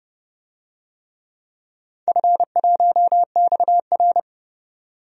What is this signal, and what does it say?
Morse code, heard as a keyed tone, about 30 wpm, 700 Hz. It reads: F1XR
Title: DXCC-100 Call Signs - 4 Characters - 30wpm